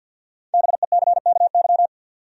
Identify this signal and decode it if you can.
Morse code, heard as a keyed tone, about 46 wpm, 700 Hz. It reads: BECKY
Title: USA Common Names - Max 5 Letters - 50wpm